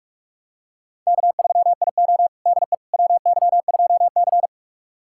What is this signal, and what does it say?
Morse code, heard as a keyed tone, about 43 wpm, 700 Hz. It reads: K3IO DE WY2C
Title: USA Callsign DE Callsign - 4 Characters - 45wpm